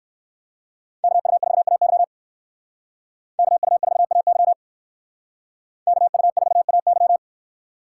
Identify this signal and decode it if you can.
Morse code, heard as a keyed tone, about 56 wpm, 700 Hz. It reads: KU4AY KU4AY KU4AY
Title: Speed Racing - USA 2x2 Call Signs 45wpm